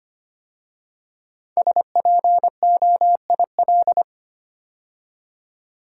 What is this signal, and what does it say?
Morse code, heard as a keyed tone, about 25 wpm, 700 Hz. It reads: SPOIL
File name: 5 Letter Words - First 3 Letters Given - 25wpm